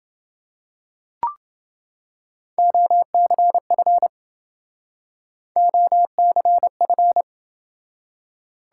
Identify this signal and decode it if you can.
Morse code, heard as a keyed tone, about 29 wpm, 700 Hz. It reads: OCF OCF